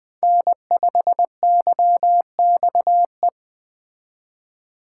Morse code, heard as a keyed tone, about 21 wpm, 700 Hz. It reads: N5YXE
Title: Speed Racing - USA 1x3 Call Signs 17wpm